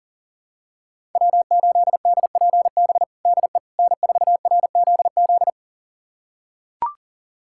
Morse code, heard as a keyed tone, about 40 wpm, 700 Hz. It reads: W8DPB DE N4RZZ
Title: USA Callsign DE Callsign - 5 Characters - 40wpm